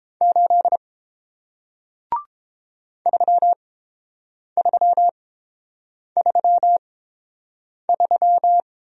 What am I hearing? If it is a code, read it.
Morse code, heard as a keyed tone, about 31 wpm, spaced as for 7 wpm, 700 Hz. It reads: 8 3333